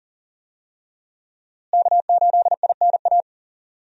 Morse code, heard as a keyed tone, about 40 wpm, 700 Hz. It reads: K8INA
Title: USA Call Signs - 5 Characters - Repeated 2x - 40wpm